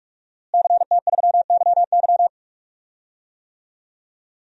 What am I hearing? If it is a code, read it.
Morse code, heard as a keyed tone, about 46 wpm, 700 Hz. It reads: CT3YY
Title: DXCC-100 Call Signs - 5 Characters - Repeated 3x - 45wpm